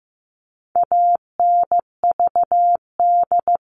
Morse code, heard as a keyed tone, about 15 wpm, 700 Hz. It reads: ANVD
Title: All Cut Numbers - 4-Digit 15wpm